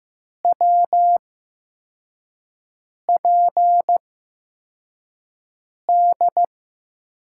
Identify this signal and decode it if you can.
Morse code, heard as a keyed tone, about 15 wpm, spaced as for 4 wpm, 700 Hz. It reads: WPD
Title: USA Call Signs - 6 Characters - 8x Character Spacing - 15wpm